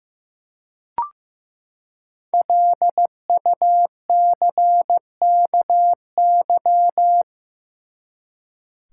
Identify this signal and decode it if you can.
Morse code, heard as a keyed tone, about 15 wpm, 700 Hz. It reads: LUCKY
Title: Five Letter Words - 15wpm